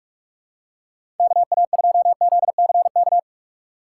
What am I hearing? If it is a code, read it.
Morse code, heard as a keyed tone, about 45 wpm, 700 Hz. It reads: KA2ZCK
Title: USA Call Signs - 4x Character Spacing - 45wpm